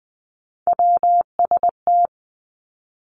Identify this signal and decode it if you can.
Morse code, heard as a keyed tone, about 20 wpm, 700 Hz. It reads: WST
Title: Three Challenging Letter-Number Combinations 20wpm